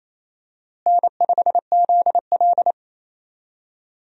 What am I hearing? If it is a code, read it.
Morse code, heard as a keyed tone, about 28 wpm, 700 Hz. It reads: N5ZL
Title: USA Call Signs - 4 Characters - Repeated 2x - 28wpm